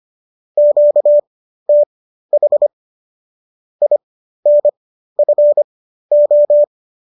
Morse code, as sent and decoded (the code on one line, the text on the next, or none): --.- - .... / .. -. ..-. ---
QTH INFO